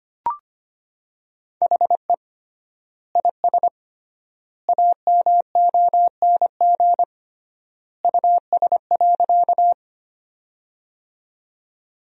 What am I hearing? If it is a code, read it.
Morse code, heard as a keyed tone, about 25 wpm, 700 Hz. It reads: HE IS AMONG US.
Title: 3x Word-Spacing - Sentences from Top 500 Words - 25wpm